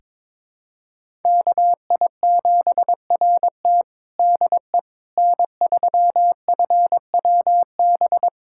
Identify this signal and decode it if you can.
Morse code, heard as a keyed tone, about 22 wpm, 700 Hz. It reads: KI7RT DE N3FWB
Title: USA Callsign DE Callsign - 5 Characters - 22wpm